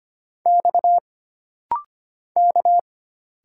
Morse code, heard as a keyed tone, about 25 wpm, 700 Hz. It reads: X K
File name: Single Character (Letter:Number:Punctuation:Prosign) - Rapid-Fire - 25wpm